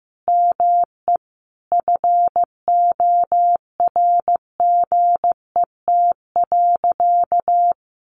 Morse code, heard as a keyed tone, about 15 wpm, 700 Hz. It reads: ME FORGET.
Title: Sentences from Top 1500 Words 15wpm